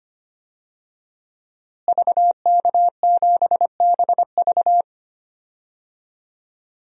Morse code, heard as a keyed tone, about 25 wpm, 700 Hz. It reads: VK7BV
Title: DXCC-100 Call Signs - 5 Characters - Repeated 2x - 25wpm